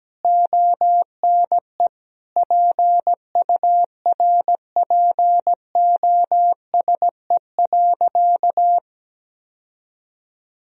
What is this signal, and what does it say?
Morse code, heard as a keyed tone, about 17 wpm, 700 Hz. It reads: ONE PURPOSE.
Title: Sentences from Top 1000 Words - Review Part 2 - 17wpm